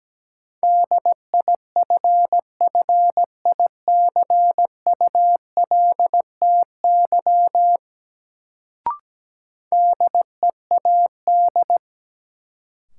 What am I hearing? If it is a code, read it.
Morse code, heard as a keyed tone, about 17 wpm, 700 Hz. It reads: DIFFICULTY DEAD
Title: Sentences from Top 1500 Words - New Word Drill Alt - 17wpm